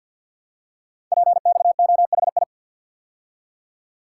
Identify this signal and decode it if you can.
Morse code, heard as a keyed tone, about 52 wpm, 700 Hz. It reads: PXOHI
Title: Code Groups - 5 Letters - 50wpm